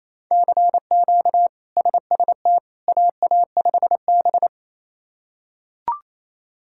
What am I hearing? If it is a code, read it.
Morse code, heard as a keyed tone, about 27 wpm, 700 Hz. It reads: CQ SST AA5B
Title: K1USN SST QSO Practice - 28wpm